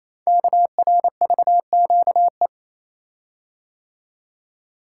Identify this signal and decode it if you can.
Morse code, heard as a keyed tone, about 29 wpm, 700 Hz. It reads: KRVQE
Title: Code Groups - 5 Letters - 28wpm